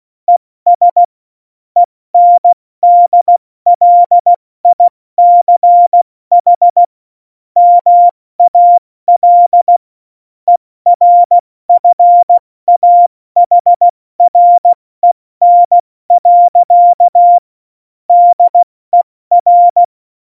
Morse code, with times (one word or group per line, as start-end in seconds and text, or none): 0.3-1.0 ES
1.8-6.8 ENDLICH
7.6-9.8 MAL
10.5-17.4 ERFAHREN.
18.1-19.8 DER